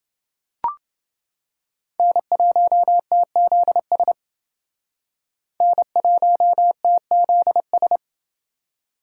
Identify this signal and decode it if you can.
Morse code, heard as a keyed tone, about 28 wpm, 700 Hz. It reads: N1TZS N1TZS